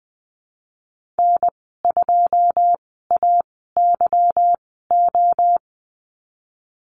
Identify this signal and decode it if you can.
Morse code, heard as a keyed tone, about 20 wpm, spaced as for 15 wpm, 700 Hz. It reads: N2AYO
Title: 213 Lesson 34g - Number 8 - Call Signs 20wpm